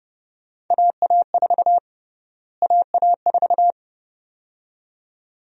Morse code, heard as a keyed tone, about 31 wpm, 700 Hz. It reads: AA4 AA4
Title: USA Call Sign Prefixes - Repeated 2x - 30wpm